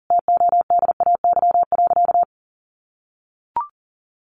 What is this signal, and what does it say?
Morse code, heard as a keyed tone, about 40 wpm, 700 Hz. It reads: TODAY.
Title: Additional Easy Phrases 40wpm